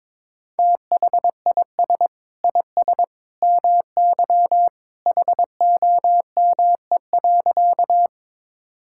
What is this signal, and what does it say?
Morse code, heard as a keyed tone, about 21 wpm, 700 Hz. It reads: THIS IS MY HOME.
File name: practice / 22wpm / Sentences with 2-Word Phrases - 22wpm